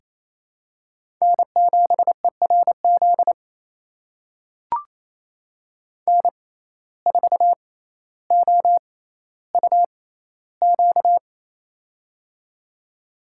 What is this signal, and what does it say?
Morse code, heard as a keyed tone, about 28 wpm, spaced as for 14 wpm, 700 Hz. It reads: N7ERZ N4OUQ
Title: USA Call Signs - 5 Characters - 6x Character Spacing - 28wpm